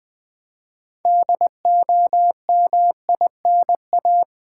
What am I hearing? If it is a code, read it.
Morse code, heard as a keyed tone, about 20 wpm, 700 Hz. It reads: DOMINA
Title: Build A Word - 2 Letters At A Time - 8-Letter Words - 20wpm